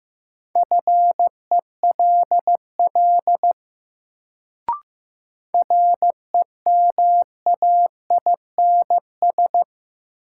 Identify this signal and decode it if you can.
Morse code, heard as a keyed tone, about 15 wpm, 700 Hz. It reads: FELL REMAINS
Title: Sets of 5 Words Encore - New Word Drill - 15wpm